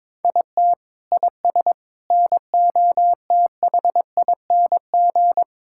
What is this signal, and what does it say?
Morse code, heard as a keyed tone, about 22 wpm, 700 Hz. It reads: IT IS NOTHING
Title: Sentences from Top 1600 Words 22wpm